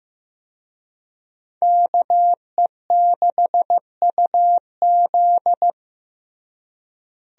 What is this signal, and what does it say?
Morse code, heard as a keyed tone, about 16 wpm, 700 Hz. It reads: KE6UZ
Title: USA Call Signs - 5 Characters - Repeated 3x - 15wpm